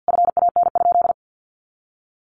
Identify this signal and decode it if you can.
Morse code, heard as a keyed tone, about 50 wpm, 700 Hz. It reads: FAN?
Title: Sentences from Top 1300 Words 50wpm